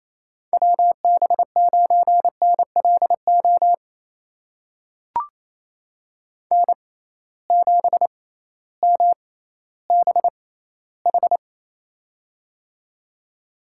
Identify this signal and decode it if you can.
Morse code, heard as a keyed tone, about 27 wpm, spaced as for 15 wpm, 700 Hz. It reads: WB9NLO N7MBH
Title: USA Call Signs - 6x Character Spacing - 28wpm